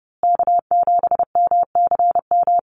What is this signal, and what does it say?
Morse code, heard as a keyed tone, about 29 wpm, 700 Hz. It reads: K7MCM
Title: USA Call Signs Repeated 3x 30wpm